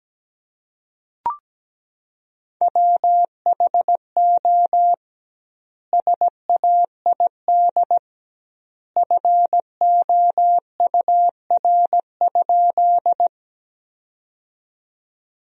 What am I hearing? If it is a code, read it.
Morse code, heard as a keyed tone, about 17 wpm, 700 Hz. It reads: WHO SAID FOUR?
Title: 2x Word-Spacing - Sentences from Top 300 Words - 17wpm